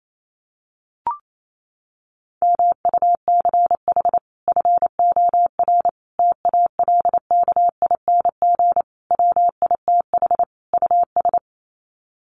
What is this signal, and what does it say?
Morse code, heard as a keyed tone, about 28 wpm, 700 Hz. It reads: MUCH FOR TALKING WITH US
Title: Sets of 5 Words Encore 28wpm